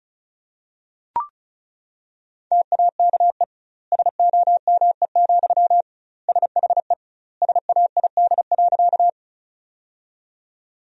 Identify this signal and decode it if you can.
Morse code, heard as a keyed tone, about 36 wpm, 700 Hz. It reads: TAKE SOME, SHE SAID.
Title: 2x Word-Spacing - Sentences from Top 200 Words - 35wpm